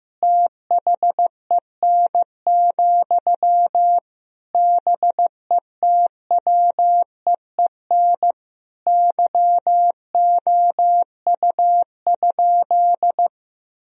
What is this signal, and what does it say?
Morse code, heard as a keyed tone, about 14 wpm, 700 Hz. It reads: THEN, BETWEEN YOU?
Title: Sentences from Top 1000 Words - Review Part 4 - 15wpm